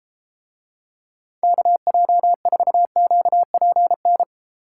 Morse code, heard as a keyed tone, about 33 wpm, 700 Hz. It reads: KJ4QPN